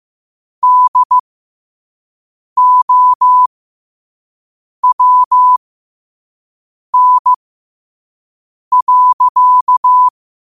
Morse code, heard as a keyed tone, about 15 wpm, spaced as for 5 wpm, 1000 Hz. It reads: DOWN.
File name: qotdfiles / 5wpm